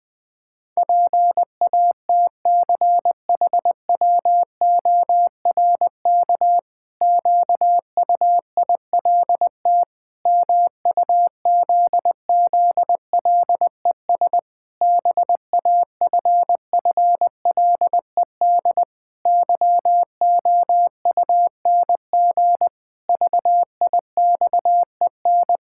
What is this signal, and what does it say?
Morse code, heard as a keyed tone, about 21 wpm, 700 Hz. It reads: PATCHWORK QUILT MUZZLES BAFFLED YOUNG VIXEN
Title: Pangram Challenge v2 - 20wpm